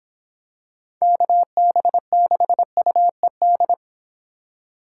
Morse code, heard as a keyed tone, about 26 wpm, 700 Hz. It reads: KB6UED